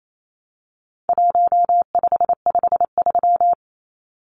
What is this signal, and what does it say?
Morse code, heard as a keyed tone, about 28 wpm, 700 Hz. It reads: 1553
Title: Numbers - 4-Digit 28wpm